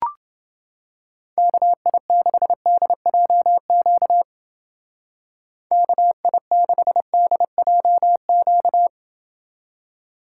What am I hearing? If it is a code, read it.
Morse code, heard as a keyed tone, about 29 wpm, 700 Hz. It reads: KI6DJQ KI6DJQ